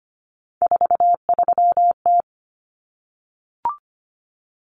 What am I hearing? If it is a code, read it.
Morse code, heard as a keyed tone, about 25 wpm, 700 Hz. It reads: 43T
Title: Common Cut Numbers - 3-Digit 25wpm